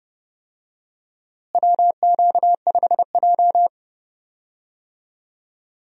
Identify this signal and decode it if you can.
Morse code, heard as a keyed tone, about 31 wpm, 700 Hz. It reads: WQ5J
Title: USA Call Signs - 4 Characters - Repeated 2x - 30wpm